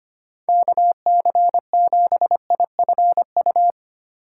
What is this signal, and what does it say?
Morse code, heard as a keyed tone, about 25 wpm, 700 Hz. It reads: KC7IFU
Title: USA Call Signs - 6 Characters - 6x Character Spacing - 25wpm